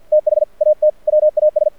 Morse code, heard as a keyed tone, about 48 wpm, 610 Hz. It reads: THAT WAS
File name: archived / CRSnet